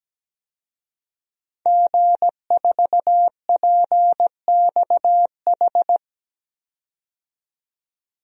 Morse code, heard as a keyed tone, about 17 wpm, 700 Hz. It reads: G4PXH
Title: DXCC-100 Call Signs - 5 Characters - 17wpm